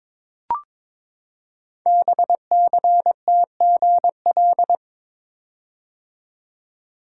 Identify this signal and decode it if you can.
Morse code, heard as a keyed tone, about 23 wpm, 700 Hz. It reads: BCTGL
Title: Code Groups - 5 Letters - 22wpm